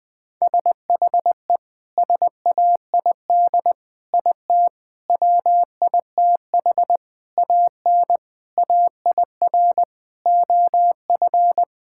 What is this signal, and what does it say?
Morse code, heard as a keyed tone, about 20 wpm, 700 Hz. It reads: SHE SAID IT WITH AN AIR OF
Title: Sentences from Top 1600 Words 20wpm